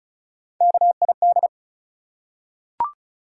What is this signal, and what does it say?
Morse code, heard as a keyed tone, about 36 wpm, 700 Hz. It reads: KID
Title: Three Letter Words - 35wpm